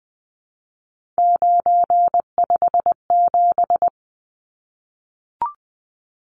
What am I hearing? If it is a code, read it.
Morse code, heard as a keyed tone, about 20 wpm, 700 Hz. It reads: 957
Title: Numbers - 3-Digit 20wpm